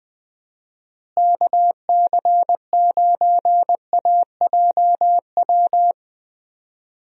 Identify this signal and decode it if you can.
Morse code, heard as a keyed tone, about 20 wpm, 700 Hz. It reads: KC9AJW
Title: USA Call Signs - 6 Characters - Repeated 3x - 20wpm